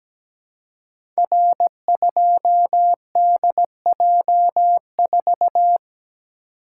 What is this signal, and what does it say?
Morse code, heard as a keyed tone, about 17 wpm, 700 Hz. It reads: R2DJ4
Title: Code Groups - 5 Letter:Numbers - 17wpm